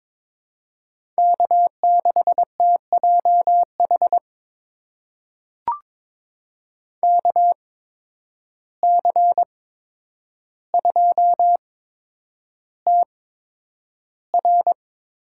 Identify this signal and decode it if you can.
Morse code, heard as a keyed tone, about 22 wpm, spaced as for 9 wpm, 700 Hz. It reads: K6TJH KC2TR
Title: USA Call Signs - 8x Character Spacing - 22wpm